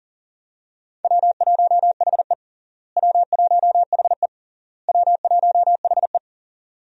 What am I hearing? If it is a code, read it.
Morse code, heard as a keyed tone, about 40 wpm, 700 Hz. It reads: W1HE W1HE W1HE
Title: USA Call Signs - 4 Characters - Repeated 3x - 40wpm